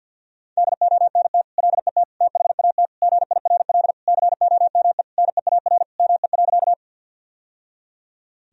Morse code, heard as a keyed tone, about 51 wpm, 700 Hz. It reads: DONT LET THAT GIRL COME NEAR ME.
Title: Sentences from Top 1000 Words - Review Part 1 - 50wpm